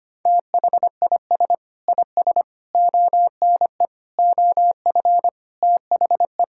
Morse code, heard as a keyed tone, about 25 wpm, 700 Hz. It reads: THIS IS ONE OF THE
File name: practice / 25wpm / Sentences with 4-Word Phrases - 25wpm